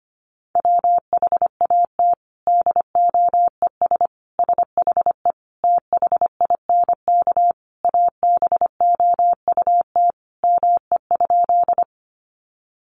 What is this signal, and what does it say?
Morse code, heard as a keyed tone, about 25 wpm, 700 Hz. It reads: WHAT DOES SHE THINK ABOUT ME?
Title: Sentences from Top 200 Words 25wpm